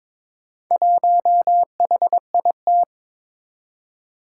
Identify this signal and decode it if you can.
Morse code, heard as a keyed tone, about 22 wpm, 700 Hz. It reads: 1HIT
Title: Four Challenging Letter-Number Combinations 22wpm